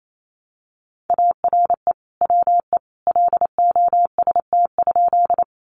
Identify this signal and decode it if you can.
Morse code, heard as a keyed tone, about 28 wpm, 700 Hz. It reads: ARE WE LOST?
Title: Sentences from Top 1400 Words 28wpm